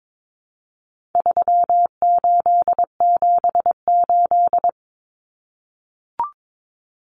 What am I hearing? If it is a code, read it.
Morse code, heard as a keyed tone, about 22 wpm, 700 Hz. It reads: 3878
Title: Numbers - 4-Digit 22wpm